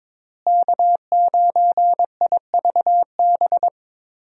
Morse code, heard as a keyed tone, about 22 wpm, 700 Hz. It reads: K9IVB
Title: USA Call Signs - 5 Characters - Repeated 3x - 22wpm